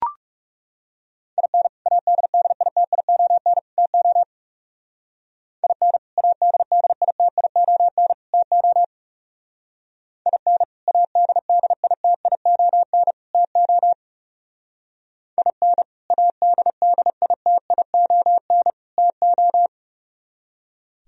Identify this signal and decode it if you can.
Morse code, heard as a keyed tone, about 36 wpm, 700 Hz. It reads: IN ADDITION TO IN ADDITION TO IN ADDITION TO IN ADDITION TO
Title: Speed-Racing - Sets of 3 Words 30wpm